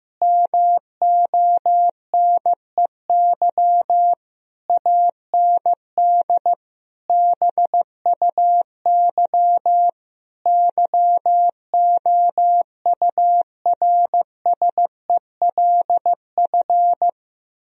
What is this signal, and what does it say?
Morse code, heard as a keyed tone, about 15 wpm, 700 Hz. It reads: MONEY AND BUY YOURSELF
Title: Sentences from Top 1000 Words - Review Part 4 - 15wpm